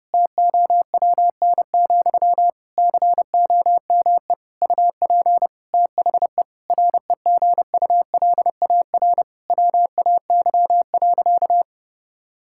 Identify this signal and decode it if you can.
Morse code, heard as a keyed tone, about 30 wpm, 700 Hz. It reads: TOWN, COME UP THE REGULAR WAY.
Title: Sentences from Top 1000 Words - Review Part 6 - 30wpm